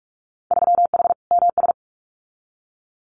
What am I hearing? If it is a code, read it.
Morse code, heard as a keyed tone, about 46 wpm, 700 Hz. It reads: 3H MS
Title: Winter Field Day Exchanges Encore 45wpm